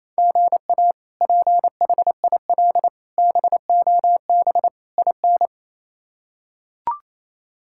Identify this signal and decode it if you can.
Morse code, heard as a keyed tone, about 28 wpm, 700 Hz. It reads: GA PHIL BOB IN
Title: K1USN SST QSO Practice - 28wpm